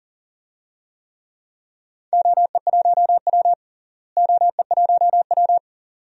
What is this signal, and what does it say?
Morse code, heard as a keyed tone, about 40 wpm, 700 Hz. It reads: OE1W OE1W
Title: DXCC-100 Call Signs - 4 Characters - Repeated 2x - 40wpm